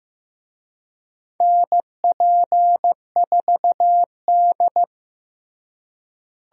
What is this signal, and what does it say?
Morse code, heard as a keyed tone, about 15 wpm, 700 Hz. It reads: NP4D
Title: USA Call Signs - 4 Characters - Repeated 3x - 15wpm